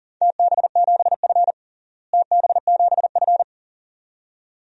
Morse code, heard as a keyed tone, about 40 wpm, 700 Hz. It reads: TB7F TB7F
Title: DXCC-100 Call Signs - 4 Characters - Repeated 2x - 40wpm